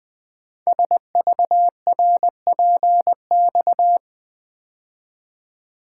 Morse code, heard as a keyed tone, about 20 wpm, 700 Hz. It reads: SVRPX
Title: Code Groups - 5 Letters - 20wpm